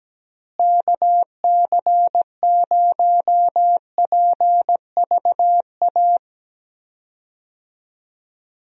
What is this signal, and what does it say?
Morse code, heard as a keyed tone, about 17 wpm, 700 Hz. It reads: KC0PVA
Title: USA Call Signs - 6 Characters - Repeated 2x - 17wpm